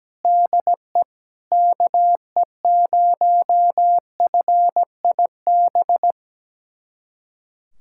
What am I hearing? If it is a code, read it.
Morse code, heard as a keyed tone, about 17 wpm, 700 Hz. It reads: DE KE0FIB
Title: USA Callsign DE Callsign - 6 Characters - 17wpm